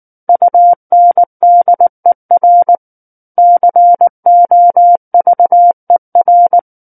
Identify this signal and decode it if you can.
Morse code, heard as a keyed tone, about 19 wpm, 700 Hz. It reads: UNDER COVER
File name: Book One / 10-Chapter 10 - 19 WPM